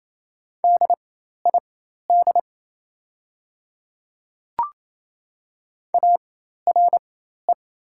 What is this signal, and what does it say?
Morse code, heard as a keyed tone, about 27 wpm, spaced as for 13 wpm, 700 Hz. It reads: DID ARE
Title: 4x Character Spacing - Top 200 Words - 28wpm